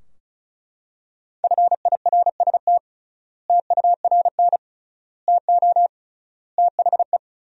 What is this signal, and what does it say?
Morse code, heard as a keyed tone, about 36 wpm, 700 Hz. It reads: FIRST TURN TO THE
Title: 3x Word-Spacing - Sentences from Top 300 Words - 35wpm